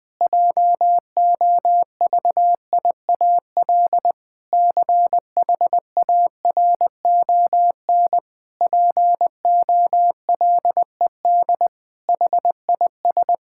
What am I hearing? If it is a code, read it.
Morse code, heard as a keyed tone, about 20 wpm, 700 Hz. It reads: JOVIAL CHARON POLED HIS
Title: Pangram Challenge v2 - 20wpm